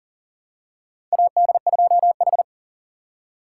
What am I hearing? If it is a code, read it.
Morse code, heard as a keyed tone, about 40 wpm, 700 Hz. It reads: AD2H
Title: USA Call Signs - 4 Characters - 3x Character Spacing - 40wpm